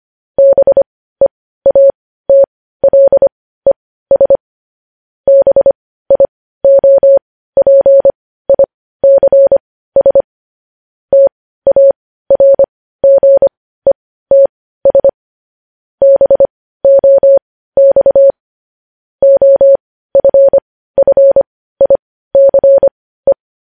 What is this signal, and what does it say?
Morse code, heard as a keyed tone, about 25 wpm, spaced as for 15 wpm, 550 Hz. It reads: BEATLES BIOPICS TARGETS BOX OFFICE
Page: news-headlines-in-morse-code-at-15-wpm